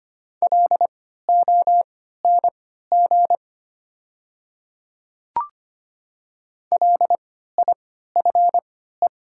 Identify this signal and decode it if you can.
Morse code, heard as a keyed tone, about 25 wpm, spaced as for 14 wpm, 700 Hz. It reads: LONG LIFE
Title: 3x Character Spacing - Top 300 Words - 25wpm